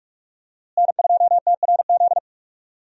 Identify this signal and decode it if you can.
Morse code, heard as a keyed tone, about 46 wpm, 700 Hz. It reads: N2TRZ
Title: USA Call Signs - 5 Characters - Repeated 2x - 45wpm